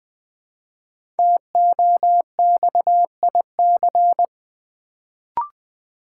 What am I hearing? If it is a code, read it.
Morse code, heard as a keyed tone, about 20 wpm, 700 Hz. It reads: TOXIC
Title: Build A Word - 1 Letter At A Time - 5-Letter Words - 20wpm